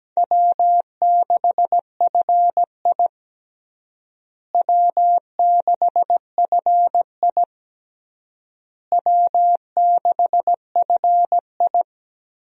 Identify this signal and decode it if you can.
Morse code, heard as a keyed tone, about 17 wpm, 700 Hz. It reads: W6FI W6FI W6FI
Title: USA Call Signs - 4 Characters - Repeated 3x - 17wpm